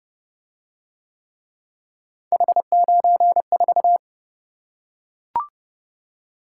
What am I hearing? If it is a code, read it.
Morse code, heard as a keyed tone, about 26 wpm, 700 Hz. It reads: H94